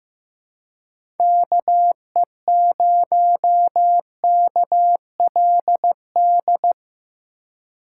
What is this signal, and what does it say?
Morse code, heard as a keyed tone, about 15 wpm, 700 Hz. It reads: KE0KLD
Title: USA Call Signs - 6 Characters - 2x Character Spacing - 15wpm